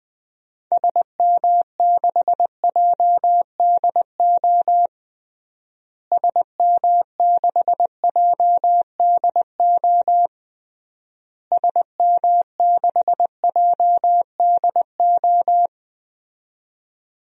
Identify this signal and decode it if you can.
Morse code, heard as a keyed tone, about 21 wpm, 700 Hz. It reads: SM6JDO SM6JDO SM6JDO
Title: DXCC-100 Call Signs - 6 Characters - Repeated 3x - 20wpm